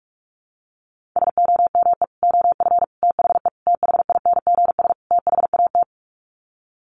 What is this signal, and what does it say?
Morse code, heard as a keyed tone, about 44 wpm, 700 Hz. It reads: SOME OF THE THINGS THAT
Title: Sets of 5 Words 45wpm